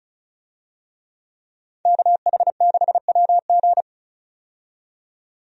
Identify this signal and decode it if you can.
Morse code, heard as a keyed tone, about 36 wpm, 700 Hz. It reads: KH6WG
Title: DXCC-100 Call Signs - 5 Characters - 35wpm